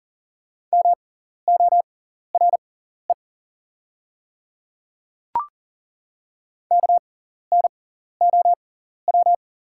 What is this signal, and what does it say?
Morse code, heard as a keyed tone, about 40 wpm, spaced as for 14 wpm, 690 Hz. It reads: MORE KNOW
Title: 6x Character Spacing - Top 200 Words - 40wpm